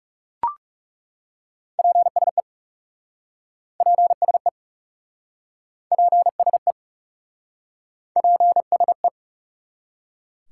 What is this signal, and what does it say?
Morse code, heard as a keyed tone, about 40 wpm, 700 Hz. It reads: PSE PSE PSE PSE